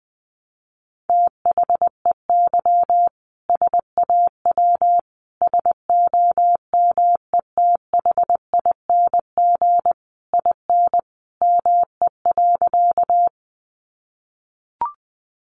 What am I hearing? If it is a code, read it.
Morse code, heard as a keyed tone, about 20 wpm, 700 Hz. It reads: THEY SAW SOMETHING IN ME.
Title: Sentences from Top 600 Words 20wpm